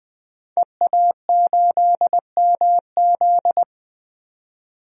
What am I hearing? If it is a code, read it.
Morse code, heard as a keyed tone, about 20 wpm, 700 Hz. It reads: EA8MZ
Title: DXCC-100 Call Signs - 5 Characters - 20wpm